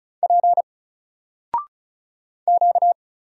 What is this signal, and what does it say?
Morse code, heard as a keyed tone, about 34 wpm, 700 Hz. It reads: P Q
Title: Single Letter-Number 35wpm